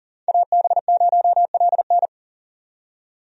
Morse code, heard as a keyed tone, about 40 wpm, 700 Hz. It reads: AB0LN
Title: USA Call Signs - 5 Characters - 40wpm